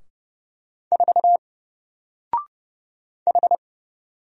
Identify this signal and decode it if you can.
Morse code, heard as a keyed tone, about 30 wpm, 700 Hz. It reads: 4 H
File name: Single-Letter-Number-30wpm